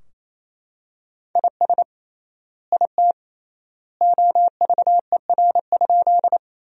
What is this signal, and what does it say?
Morse code, heard as a keyed tone, about 28 wpm, 700 Hz. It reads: IS IT OVER?
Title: 3x Word-Spacing - Sentences from Top 100 Words - 28wpm